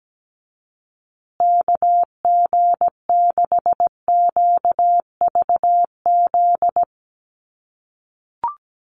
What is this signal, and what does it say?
Morse code, heard as a keyed tone, about 17 wpm, 700 Hz. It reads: KG6QVZ
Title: USA Call Signs Repeated 3x 17wpm